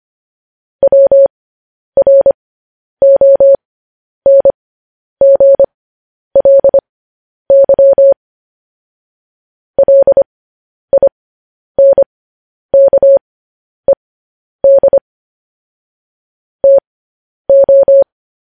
Morse code, heard as a keyed tone, about 25 wpm, spaced as for 10 wpm, 550 Hz. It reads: WRONGLY LINKED TO
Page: news headlines in morse code at 10 wpm